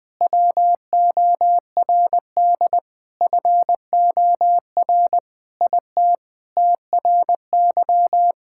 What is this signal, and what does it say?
Morse code, heard as a keyed tone, about 20 wpm, 700 Hz. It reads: WORD FOR IT TRY
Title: Sentences with 7-Word Phrases - 20wpm